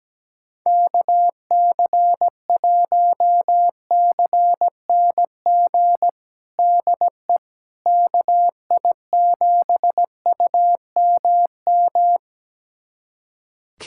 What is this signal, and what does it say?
Morse code, heard as a keyed tone, about 17 wpm, 700 Hz. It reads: KC1CNG DE KI7UMM
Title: USA Callsign DE Callsign 17wpm